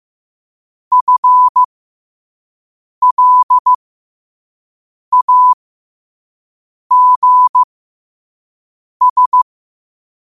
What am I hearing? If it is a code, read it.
Morse code, heard as a keyed tone, about 15 wpm, spaced as for 5 wpm, 1000 Hz. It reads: FLAGS